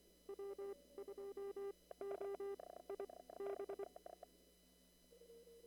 Morse code, heard as a keyed tone, about 23 wpm, spaced as for 17 wpm, 400 Hz. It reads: W2OIB
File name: CW / kt5j wpx cw-op-k5pi